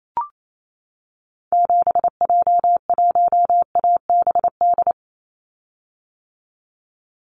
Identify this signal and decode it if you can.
Morse code, heard as a keyed tone, about 28 wpm, 700 Hz. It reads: 7J1ABD
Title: Call Signs - 28wpm